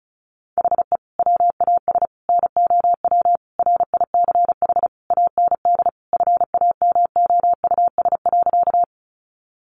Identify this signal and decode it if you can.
Morse code, heard as a keyed tone, about 34 wpm, 700 Hz. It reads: HE WAS NOW RICH AND FAMOUS.
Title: Sentences from Top 1500 Words 35wpm